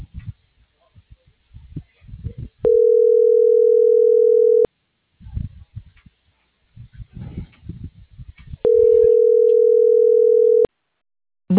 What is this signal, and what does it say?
Ringback tone (North America), 440 + 480 Hz.